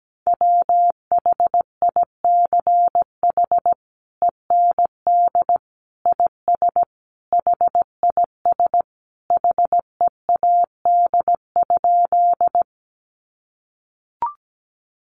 Morse code, heard as a keyed tone, about 17 wpm, 700 Hz. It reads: WHICH END IS HIS HEAD?
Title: Sentences from Top 300 Words 17wpm